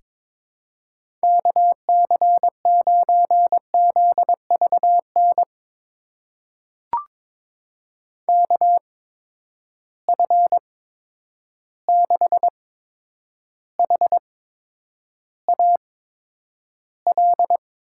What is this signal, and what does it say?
Morse code, heard as a keyed tone, about 22 wpm, spaced as for 9 wpm, 700 Hz. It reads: KC9ZVN KF6HAL
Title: USA Call Signs - 6 Characters - 8x Character Spacing - 22wpm